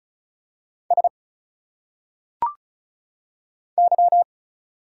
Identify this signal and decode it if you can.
Morse code, heard as a keyed tone, about 36 wpm, 700 Hz. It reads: S Y
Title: Single Letters - Phonetic Pronunciation 35wpm